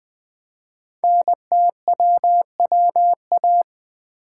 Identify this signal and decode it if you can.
Morse code, heard as a keyed tone, about 20 wpm, 700 Hz. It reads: NTWWA
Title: Code Groups - 5 Letters - 20wpm